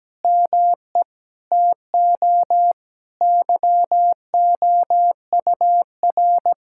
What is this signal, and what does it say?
Morse code, heard as a keyed tone, about 17 wpm, 700 Hz. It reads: ME TO YOUR
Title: Sentences from Top 1000 Words - Review Part 6 - 17wpm